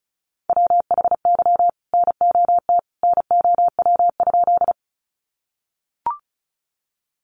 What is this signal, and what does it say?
Morse code, heard as a keyed tone, about 34 wpm, 700 Hz. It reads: WHY NOT NOW?
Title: Sentences from Top 600 Words 35wpm